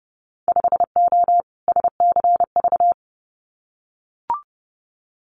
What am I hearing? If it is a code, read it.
Morse code, heard as a keyed tone, about 30 wpm, 700 Hz. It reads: 5O SCV
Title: Winter Field Day Exchanges Encore 30wpm